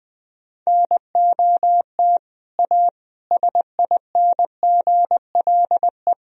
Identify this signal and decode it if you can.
Morse code, heard as a keyed tone, about 20 wpm, 700 Hz. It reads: NOT A SINGLE
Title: Sentences from Top 1000 Words - Review Part 4 - 20wpm